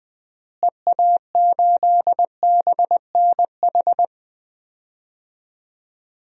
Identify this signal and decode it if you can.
Morse code, heard as a keyed tone, about 20 wpm, 700 Hz. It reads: EA8BNH
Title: DXCC-100 Call Signs - 6 Characters - Repeated 3x - 20wpm